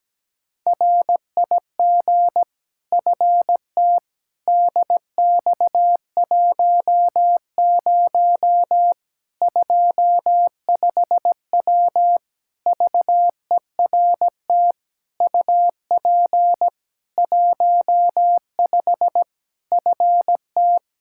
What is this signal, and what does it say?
Morse code, heard as a keyed tone, about 17 wpm, 700 Hz. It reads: RIG FT DX10 25W VERT UP 15 FT
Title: K1USN SST QSO Practice 17wpm